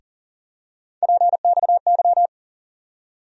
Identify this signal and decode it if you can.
Morse code, heard as a keyed tone, about 39 wpm, 700 Hz. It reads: PXY
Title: USA Call Sign Postfixes - 3 Letters 40wpm